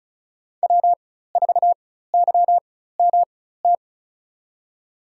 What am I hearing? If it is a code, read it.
Morse code, heard as a keyed tone, about 34 wpm, spaced as for 16 wpm, 700 Hz. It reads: W4YMT
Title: USA Call Signs - 5 Characters - 4x Character Spacing - 35wpm